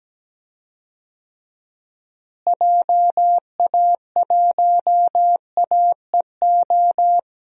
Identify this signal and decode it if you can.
Morse code, heard as a keyed tone, about 17 wpm, 700 Hz. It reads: JA1AEO
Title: DXCC-100 Call Signs - 6 Characters - Repeated 2x - 17wpm